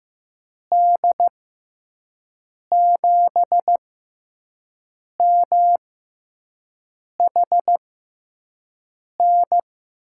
Morse code, heard as a keyed tone, about 15 wpm, spaced as for 5 wpm, 700 Hz. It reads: D7MHN
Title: USA Call Signs - 6 Characters - 6x Character Spacing - 15wpm